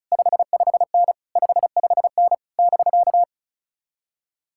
Morse code, heard as a keyed tone, about 34 wpm, 700 Hz. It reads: 55N 55N <BK>
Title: Speed-Racing Common QSO Phrases - 35wpm